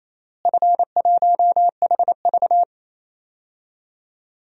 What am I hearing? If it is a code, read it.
Morse code, heard as a keyed tone, about 28 wpm, 700 Hz. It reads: F1HV
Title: DXCC-100 Call Signs - 4 Characters - Repeated 3x - 28wpm